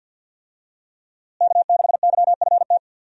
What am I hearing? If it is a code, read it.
Morse code, heard as a keyed tone, about 49 wpm, 700 Hz. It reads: K6YFT
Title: USA Call Signs - 5 Characters - Repeated 3x - 50wpm